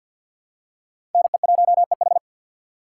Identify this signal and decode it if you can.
Morse code, heard as a keyed tone, about 47 wpm, 700 Hz. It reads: NE1EH
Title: USA Call Signs - 5 Characters - 6x Character Spacing - 50wpm